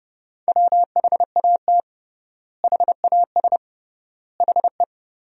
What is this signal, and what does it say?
Morse code, heard as a keyed tone, about 31 wpm, 700 Hz. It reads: WHAT HAS HE
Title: 3x Word-Spacing - Sentences from Top 500 Words - 30wpm